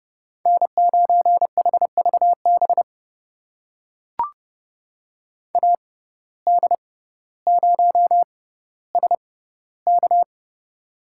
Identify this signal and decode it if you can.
Morse code, heard as a keyed tone, about 31 wpm, spaced as for 16 wpm, 700 Hz. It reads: N9HVB AD0SK
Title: USA Call Signs - 5 Characters - 6x Character Spacing - 30wpm